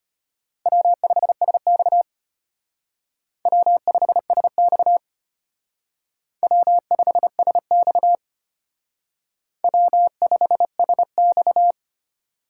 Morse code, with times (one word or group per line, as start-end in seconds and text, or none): 0.7-2.0 W5SX
3.5-5.0 W5SX
6.4-8.2 W5SX
9.6-11.7 W5SX